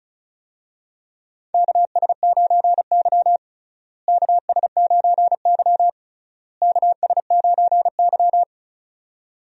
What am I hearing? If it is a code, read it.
Morse code, heard as a keyed tone, about 34 wpm, 700 Hz. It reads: KS9Y KS9Y KS9Y
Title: USA Call Signs - 4 Characters - Repeated 3x - 35wpm